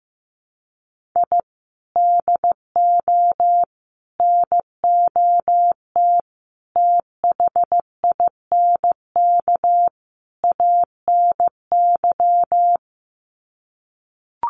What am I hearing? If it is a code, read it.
Morse code, heard as a keyed tone, about 15 wpm, 700 Hz. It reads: I DO NOT THINK ANY
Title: Sets of 5 Words Encore 15wpm